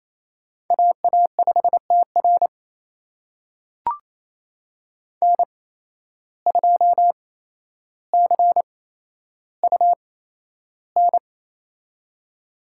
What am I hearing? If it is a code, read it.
Morse code, heard as a keyed tone, about 28 wpm, spaced as for 12 wpm, 700 Hz. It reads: AA5TR N2CUN
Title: USA Call Signs - 5 Characters - 8x Character Spacing - 28wpm